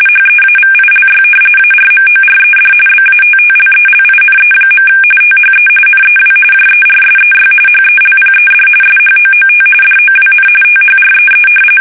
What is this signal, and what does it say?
RTTY, about 45.45 baud. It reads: HEAVILY ARMORED BATTLELINE. THE NEW JERSEY HELPED SHOOT DOWN THREE
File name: fsk850 / battleshipnj